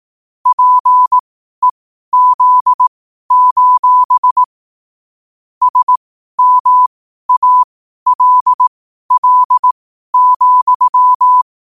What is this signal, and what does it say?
Morse code, heard as a keyed tone, about 18 wpm, spaced as for 12 wpm, 1000 Hz. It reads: PEZ: SMALL,
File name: qotdfiles / 12wpm